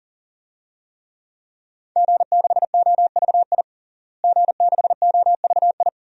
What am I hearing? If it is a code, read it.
Morse code, heard as a keyed tone, about 40 wpm, 700 Hz. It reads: G6OVI G6OVI
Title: DXCC-100 Call Signs - 5 Characters - Repeated 2x - 40wpm